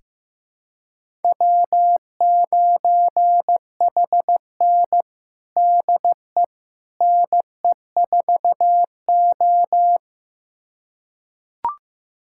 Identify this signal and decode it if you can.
Morse code, heard as a keyed tone, about 16 wpm, 700 Hz. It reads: W9HN DE NE4O
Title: USA Callsign DE Callsign - 4 Characters - 15wpm